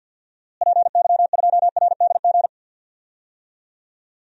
Morse code, heard as a keyed tone, about 51 wpm, 700 Hz. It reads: PY2RDG
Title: DXCC-100 Call Signs - 6 Characters - Repeated 3x - 50wpm